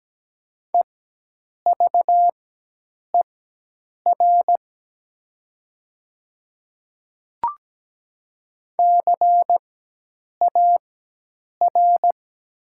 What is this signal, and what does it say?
Morse code, heard as a keyed tone, about 17 wpm, spaced as for 8 wpm, 700 Hz. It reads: EVER CAR
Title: 4x Character Spacing - Top 500 Words - 17wpm